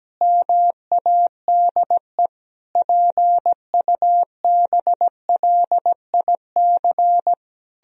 Morse code, heard as a keyed tone, about 17 wpm, 700 Hz. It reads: MADE PUBLIC
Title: Sentences from Top 1000 Words - Review Part 3 - 17wpm